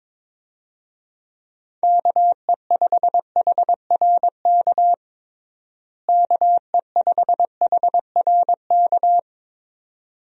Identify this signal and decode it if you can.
Morse code, heard as a keyed tone, about 21 wpm, 700 Hz. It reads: KE5HRK KE5HRK
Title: USA Call Signs - 6 Characters - Repeated 2x - 22wpm